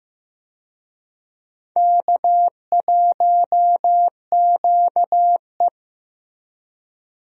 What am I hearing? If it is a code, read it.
Morse code, heard as a keyed tone, about 15 wpm, 700 Hz. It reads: K1QE
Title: USA Call Signs - 4 Characters - Repeated 2x - 15wpm